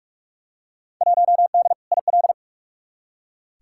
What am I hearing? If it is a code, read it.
Morse code, heard as a keyed tone, about 43 wpm, 700 Hz. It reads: 1D IL
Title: Field Day Exchanges - 45wpm